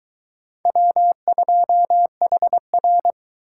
Morse code, heard as a keyed tone, about 22 wpm, 700 Hz. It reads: W2HR